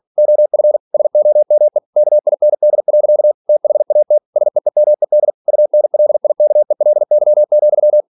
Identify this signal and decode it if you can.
Morse code, heard as a keyed tone, about 46 wpm, 600 Hz. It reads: OF SOME KIND. THAT SEEMED UNLIKELY,
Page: Never 47wpm